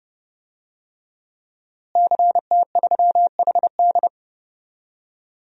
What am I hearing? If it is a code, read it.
Morse code, heard as a keyed tone, about 30 wpm, 700 Hz. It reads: CT3HD
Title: DXCC-100 Call Signs - 5 Characters - Repeated 2x - 30wpm